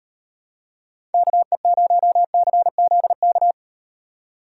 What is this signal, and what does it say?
Morse code, heard as a keyed tone, about 36 wpm, 700 Hz. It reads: KE0CZK